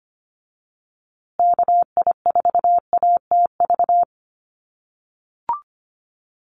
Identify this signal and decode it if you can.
Morse code, heard as a keyed tone, about 25 wpm, 700 Hz. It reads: KI4ATV
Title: USA Call Signs Repeated 3x 25wpm